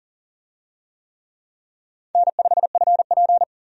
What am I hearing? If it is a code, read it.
Morse code, heard as a keyed tone, about 40 wpm, 700 Hz. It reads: N5FP
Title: USA Call Signs - 4 Characters - Repeated 2x - 40wpm